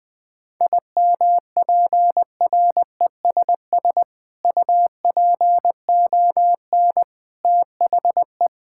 Morse code, heard as a keyed tone, about 20 wpm, 700 Hz. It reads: IMPRESS UPON THE
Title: Sentences from Top 1600 Words 20wpm